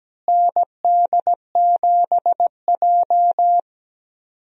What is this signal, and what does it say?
Morse code, heard as a keyed tone, about 17 wpm, 700 Hz. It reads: ND7J
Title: USA Call Signs - 4 Characters - Repeated 3x - 17wpm